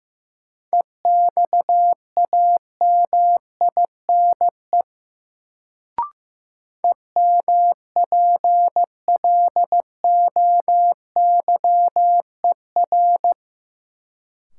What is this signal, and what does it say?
Morse code, heard as a keyed tone, about 15 wpm, 700 Hz. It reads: EXAMINE EMPLOYER
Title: Sentences from Top 1600 Words - New Word Drill Alt - 15wpm